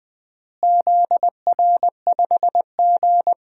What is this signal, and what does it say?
Morse code, heard as a keyed tone, about 20 wpm, 700 Hz. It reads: ZR5G
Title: DXCC-100 Call Signs - 4 Characters - Repeated 3x - 20wpm